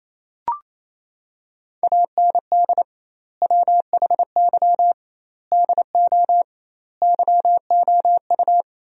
Morse code, heard as a keyed tone, about 28 wpm, 700 Hz. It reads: AND WHY DO YOU
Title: 2x Word-Spacing - Sentences from Top 400 Words - 28wpm